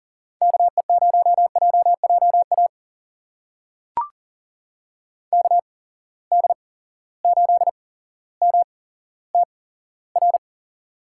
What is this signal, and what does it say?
Morse code, heard as a keyed tone, about 40 wpm, spaced as for 17 wpm, 700 Hz. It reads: KE0JJA KD8MTR
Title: USA Call Signs - 8x Character Spacing - 40wpm